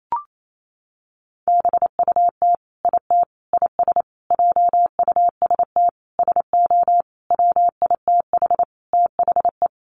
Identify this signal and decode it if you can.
Morse code, heard as a keyed tone, about 28 wpm, 700 Hz. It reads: BUT IT IS JUST SO WITH THE
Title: Sentences from Top 200 Words 28wpm